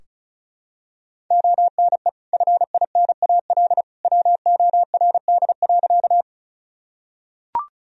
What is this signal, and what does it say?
Morse code, heard as a keyed tone, about 34 wpm, 700 Hz. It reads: ONE FINAL WORD.
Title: Sentences from Top 1000 Words - Review Part 2 - 35wpm